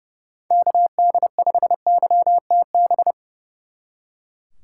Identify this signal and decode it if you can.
Morse code, heard as a keyed tone, about 38 wpm, 700 Hz. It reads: KD5YTB